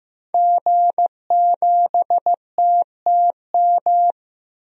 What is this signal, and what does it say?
Morse code, heard as a keyed tone, about 15 wpm, 700 Hz. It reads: G7TTM
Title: DXCC-100 Call Signs - 5 Characters - 15wpm